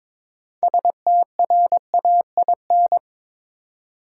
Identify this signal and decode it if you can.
Morse code, heard as a keyed tone, about 21 wpm, 700 Hz. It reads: STRAIN
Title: Six Letter Words - 22wpm